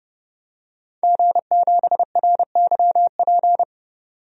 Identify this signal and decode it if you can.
Morse code, heard as a keyed tone, about 32 wpm, 700 Hz. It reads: G7RYP
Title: DXCC-100 Call Signs - 5 Characters - 30wpm